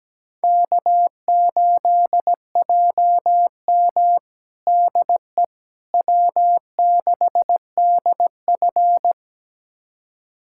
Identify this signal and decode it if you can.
Morse code, heard as a keyed tone, about 17 wpm, 700 Hz. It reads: K8JM DE W6DF
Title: USA Callsign DE Callsign - 4 Characters - 17wpm